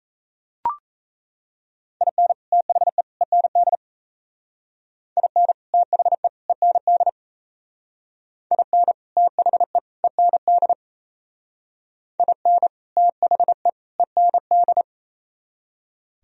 Morse code, heard as a keyed tone, about 36 wpm, 700 Hz. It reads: IN THE END IN THE END IN THE END IN THE END